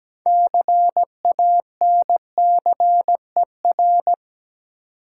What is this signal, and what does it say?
Morse code, heard as a keyed tone, about 17 wpm, 700 Hz. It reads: CANCER
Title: Six Letter Words - 17wpm